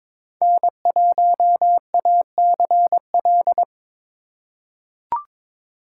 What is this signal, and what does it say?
Morse code, heard as a keyed tone, about 22 wpm, 700 Hz. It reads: N1ACL
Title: USA Call Signs - 3x Character Spacing - 22wpm